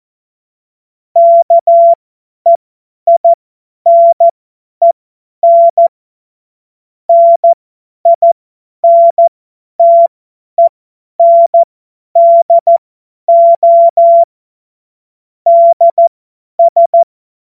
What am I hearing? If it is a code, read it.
Morse code, heard as a keyed tone, about 14 wpm, spaced as for 10 wpm, 680 Hz. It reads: KEINEN NINTENDO DS
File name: Morsekurs / Witze / Witz 02. November 2021, 50 F70